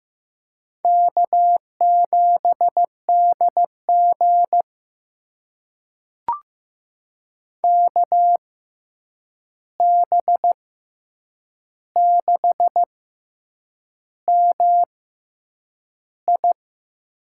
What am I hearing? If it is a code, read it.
Morse code, heard as a keyed tone, about 15 wpm, spaced as for 7 wpm, 700 Hz. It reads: K7DG KB6MI
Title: USA Call Signs - 6x Character Spacing - 15wpm